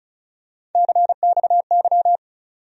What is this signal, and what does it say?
Morse code, heard as a keyed tone, about 35 wpm, 700 Hz. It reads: CXY
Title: USA Call Sign Postfixes - 3 Letters 35wpm